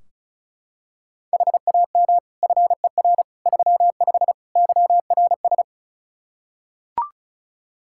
Morse code, heard as a keyed tone, about 36 wpm, 700 Hz. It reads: HAM FER 35 YRS
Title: Practice QSO - Exchange 3 - More Info - 35wpm